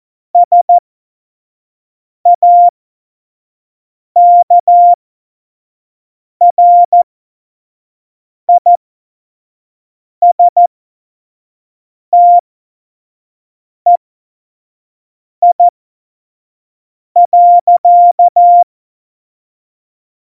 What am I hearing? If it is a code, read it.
Morse code, heard as a keyed tone, about 14 wpm, spaced as for 5 wpm, 700 Hz. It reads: SAKRISTEI.
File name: Morsekurs / Witze / Witz 26. Januar 2021, 25 F70